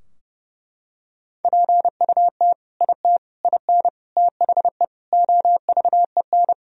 Morse code, heard as a keyed tone, about 30 wpm, 700 Hz. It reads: PUT IT IN THE OVEN
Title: Sentences from Top 1000 Words - Review Part 4 - 30wpm